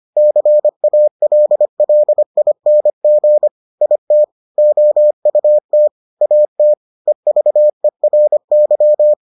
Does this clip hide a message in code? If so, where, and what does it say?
Yes: Morse code from 0.2 to 9.1 s, heard as a keyed tone, about 25 wpm, 600 Hz. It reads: CALLING IT OUT AT EVERY